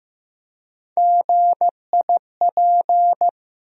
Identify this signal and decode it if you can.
Morse code, heard as a keyed tone, about 15 wpm, 700 Hz. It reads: GIP
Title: USA Call Sign Postfixes - 3 Letters 15wpm